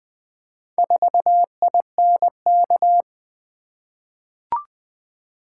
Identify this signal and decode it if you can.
Morse code, heard as a keyed tone, about 20 wpm, 700 Hz. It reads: 4INK
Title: Four Letter-Number Combinations - 20wpm